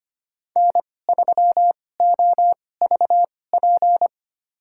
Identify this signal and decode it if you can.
Morse code, heard as a keyed tone, about 25 wpm, spaced as for 18 wpm, 700 Hz. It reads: N3OVP
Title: USA Call Signs - 5 Characters - 2x Character Spacing - 25wpm